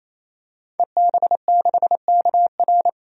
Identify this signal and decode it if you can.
Morse code, heard as a keyed tone, about 28 wpm, 700 Hz. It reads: EB6KR
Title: DXCC-100 Call Signs - 5 Characters - 28wpm